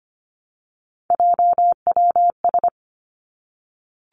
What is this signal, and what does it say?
Morse code, heard as a keyed tone, about 25 wpm, 700 Hz. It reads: JWS
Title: Three Challenging Letter-Number Combinations 25wpm